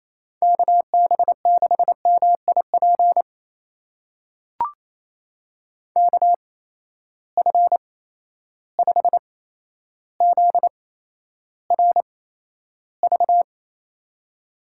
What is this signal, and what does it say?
Morse code, heard as a keyed tone, about 28 wpm, spaced as for 12 wpm, 700 Hz. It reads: KB6MIP KF5ZRV
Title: USA Call Signs - 8x Character Spacing - 28wpm